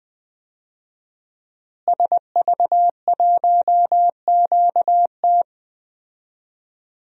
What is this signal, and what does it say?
Morse code, heard as a keyed tone, about 20 wpm, 700 Hz. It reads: SV1QT
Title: DXCC-100 Call Signs - 5 Characters - Repeated 3x - 20wpm